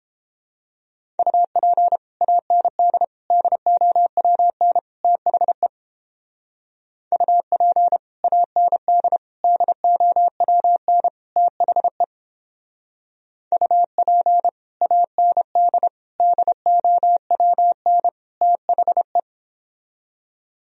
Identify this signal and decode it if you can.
Morse code, heard as a keyed tone, about 31 wpm, 700 Hz. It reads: UP AND DOWN THE UP AND DOWN THE UP AND DOWN THE